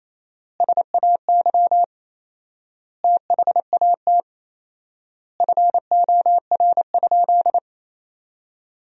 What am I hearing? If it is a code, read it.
Morse code, heard as a keyed tone, about 28 wpm, 700 Hz. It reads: SAY THAT FOR?
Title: 4x Word-Spacing - Sentences from Top 200 Words - 28wpm